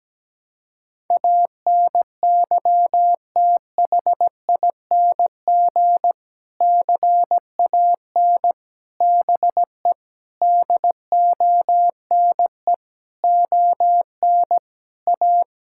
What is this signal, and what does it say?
Morse code, heard as a keyed tone, about 17 wpm, 700 Hz. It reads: ANYTHING CAN BE DONE ON A
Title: Sentences from Top 1000 Words - Review Part 5 - 17wpm